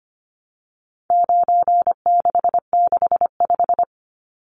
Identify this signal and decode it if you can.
Morse code, heard as a keyed tone, about 25 wpm, 700 Hz. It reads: 9665
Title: Numbers - 4-Digit 25wpm